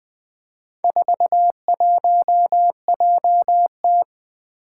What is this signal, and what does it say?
Morse code, heard as a keyed tone, about 20 wpm, 700 Hz. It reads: 41JT
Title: Four Challenging Letter-Number Combinations 20wpm